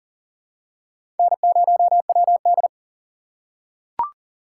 Morse code, heard as a keyed tone, about 40 wpm, 700 Hz. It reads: N0WD
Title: USA Call Signs - 4 Characters - 4x Character Spacing - 40wpm